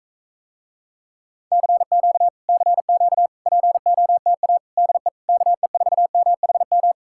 Morse code, heard as a keyed tone, about 45 wpm, 700 Hz. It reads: CQ CQ POTA DE KE4MHM